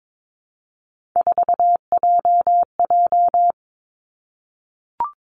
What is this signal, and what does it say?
Morse code, heard as a keyed tone, about 22 wpm, 700 Hz. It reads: 4JJ
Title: Three Challenging Letter-Number Combinations 22wpm